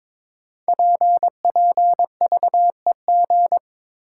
Morse code, heard as a keyed tone, about 22 wpm, 700 Hz. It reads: PPVEG
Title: Code Groups - 5 Letter:Numbers - 22wpm